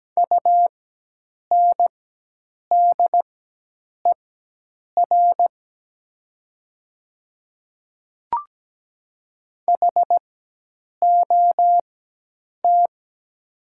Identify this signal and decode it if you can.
Morse code, heard as a keyed tone, about 17 wpm, spaced as for 8 wpm, 700 Hz. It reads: UNDER HOT
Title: 4x Character Spacing - Top 500 Words - 17wpm